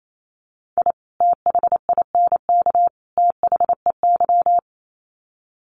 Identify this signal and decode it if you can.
Morse code, heard as a keyed tone, about 28 wpm, 700 Hz. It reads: I THINK THEY
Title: Sets of 3 Words 28wpm